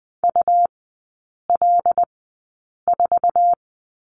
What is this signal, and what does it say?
Morse code, heard as a keyed tone, about 20 wpm, spaced as for 8 wpm, 700 Hz. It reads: UL4
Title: 100 Lesson 17e - Letter u - Three Characters 20wpm